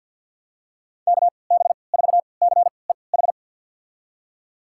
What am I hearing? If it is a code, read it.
Morse code, heard as a keyed tone, about 50 wpm, spaced as for 28 wpm, 700 Hz. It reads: KB4CEH
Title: USA Call Signs - 6 Characters - 3x Character Spacing - 50wpm